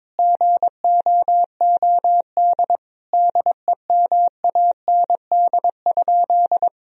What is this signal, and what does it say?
Morse code, heard as a keyed tone, about 22 wpm, 700 Hz. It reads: GOOD DEMAND?
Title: Sentences from Top 1000 Words - Review Part 4 - 22wpm